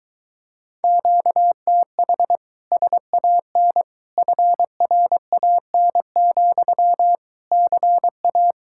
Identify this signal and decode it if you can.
Morse code, heard as a keyed tone, about 23 wpm, 700 Hz. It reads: QTH SAN FRAN, CA